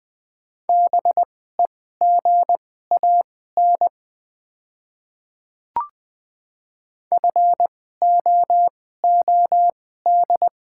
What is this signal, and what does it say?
Morse code, heard as a keyed tone, about 20 wpm, spaced as for 14 wpm, 700 Hz. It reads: BEGAN FOOD
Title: 2x Character Spacing - Top 300 Words - 20wpm